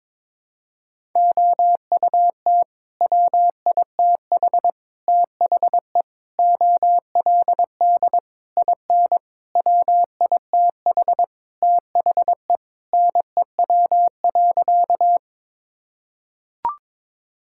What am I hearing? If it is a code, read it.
Morse code, heard as a keyed tone, about 22 wpm, 700 Hz. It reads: OUT WITH THE OLD IN WITH THE NEW.
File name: practice / 22wpm / Sentences with 8-Word Phrases - 22wpm